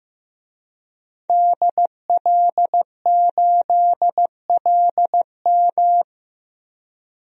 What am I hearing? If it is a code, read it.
Morse code, heard as a keyed tone, about 15 wpm, 700 Hz. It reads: DL8LM
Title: DXCC-100 Call Signs - 5 Characters - 15wpm